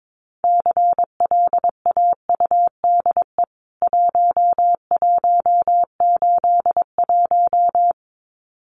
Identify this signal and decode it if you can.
Morse code, heard as a keyed tone, about 22 wpm, 700 Hz. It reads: CLAUDE 1181
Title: CWT Exchange 22wpm